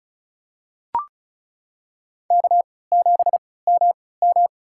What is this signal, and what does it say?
Morse code, heard as a keyed tone, about 36 wpm, spaced as for 20 wpm, 700 Hz. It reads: K7MM
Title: USA Call Signs - 3x Character Spacing - 35wpm